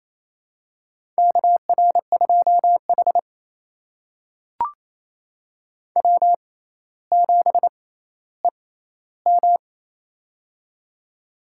Morse code, heard as a keyed tone, about 28 wpm, spaced as for 14 wpm, 700 Hz. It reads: KR2H W7EM
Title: USA Call Signs - 4 Characters - 6x Character Spacing - 28wpm